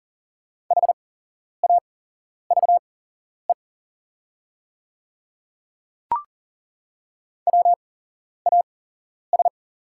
Morse code, heard as a keyed tone, about 40 wpm, spaced as for 11 wpm, 690 Hz. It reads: HAVE WAS
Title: 8x Character Spacing - Top 300 Words - 40wpm